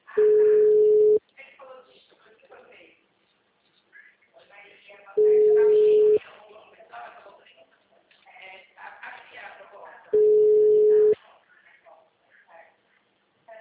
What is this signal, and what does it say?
Ringback tone (Europe, 425 Hz), with speech.